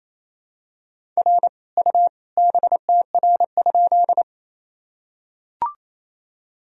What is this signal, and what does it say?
Morse code, heard as a keyed tone, about 27 wpm, 700 Hz. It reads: R U BTR?
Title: Sentences from Top 400 Words and Abbreviations 28wpm